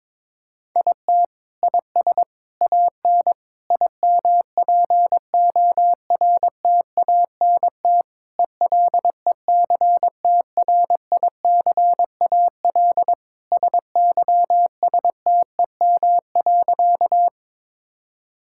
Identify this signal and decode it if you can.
Morse code, heard as a keyed tone, about 21 wpm, 700 Hz. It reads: IT IS AN IMPORTANT ELECTRICAL SYSTEM.
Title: Sentences from Top 1000 Words - Review Part 6 - 22wpm